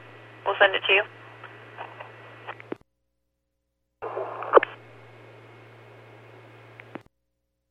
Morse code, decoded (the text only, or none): none